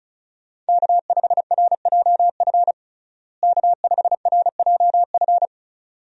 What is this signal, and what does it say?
Morse code, heard as a keyed tone, about 36 wpm, 700 Hz. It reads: K5RJF K5RJF
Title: USA Call Signs - 5 Characters - Repeated 2x - 35wpm